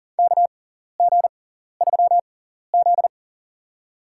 Morse code, heard as a keyed tone, about 40 wpm, spaced as for 14 wpm, 700 Hz. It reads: KG3Z
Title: USA Call Signs - 4 Characters - 6x Character Spacing - 40wpm